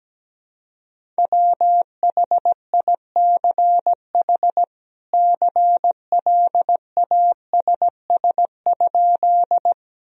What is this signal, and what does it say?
Morse code, heard as a keyed tone, about 17 wpm, 700 Hz. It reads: WHICH CLASS?
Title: Sentences from Top 1000 Words - Review Part 2 - 17wpm